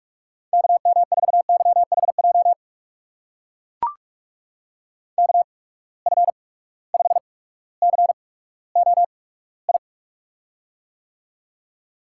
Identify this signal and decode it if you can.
Morse code, heard as a keyed tone, about 44 wpm, spaced as for 19 wpm, 700 Hz. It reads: KM4YHJ KF5COI
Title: USA Call Signs - 6 Characters - 8x Character Spacing - 45wpm